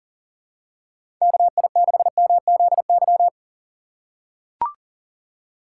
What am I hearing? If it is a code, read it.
Morse code, heard as a keyed tone, about 40 wpm, 700 Hz. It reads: KI6MZY
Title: USA Call Signs - 6 Characters - 2x Character Spacing - 40wpm